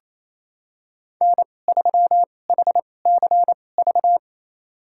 Morse code, heard as a keyed tone, about 28 wpm, spaced as for 20 wpm, 700 Hz. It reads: N3HCV
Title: USA Call Signs - 5 Characters - 2x Character Spacing - 28wpm